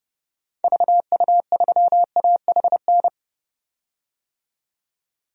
Morse code, heard as a keyed tone, about 30 wpm, 700 Hz. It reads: VU3AHN
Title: DXCC-100 Call Signs - 6 Characters - 30wpm